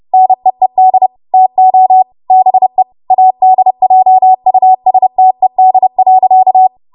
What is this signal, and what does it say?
Morse code, heard as a keyed tone, about 30 wpm, 750 Hz. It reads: NEED TO BE ADJUSTED.
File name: Archive / 30 WPM / 30WPM